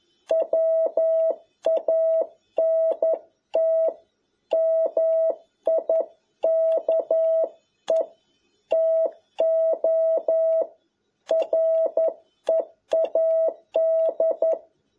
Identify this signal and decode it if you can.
Morse code, heard as a keyed tone, about 11 wpm, 650 Hz. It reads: WANT MIKE TO READ
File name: jh8cbh / 2021cw